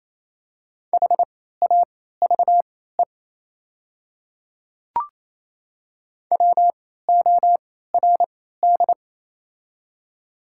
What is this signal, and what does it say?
Morse code, heard as a keyed tone, about 28 wpm, spaced as for 16 wpm, 690 Hz. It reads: HAVE WORD
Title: 3x Character Spacing - Top 100 Words - 28wpm